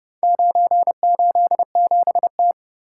Morse code, heard as a keyed tone, about 31 wpm, 700 Hz. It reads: 987T